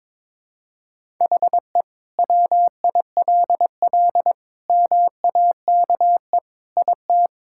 Morse code, heard as a keyed tone, about 22 wpm, 700 Hz. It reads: HE WILL MAKE IT
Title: Sentences from Top 1000 Words - Review Part 5 - 22wpm